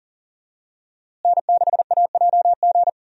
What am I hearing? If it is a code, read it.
Morse code, heard as a keyed tone, about 40 wpm, 700 Hz. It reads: N6AJG
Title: USA Call Signs - 5 Characters - Repeated 2x - 40wpm